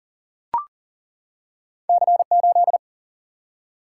Morse code, heard as a keyed tone, about 40 wpm, 700 Hz. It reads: C8
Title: Single-Letter Single-Number 40wpm